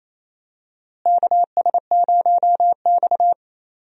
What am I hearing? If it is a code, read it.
Morse code, heard as a keyed tone, about 28 wpm, 700 Hz. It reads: KS0X
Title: USA Call Signs - 4 Characters - Repeated 2x - 28wpm